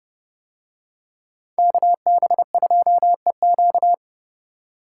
Morse code, heard as a keyed tone, about 28 wpm, 700 Hz. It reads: KB2EQ